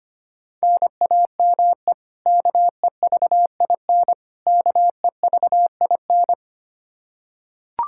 Morse code, heard as a keyed tone, about 25 wpm, 700 Hz. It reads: NAME KEVIN KEVIN
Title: Practice QSO - Exchange 1 - Beginning - 25wpm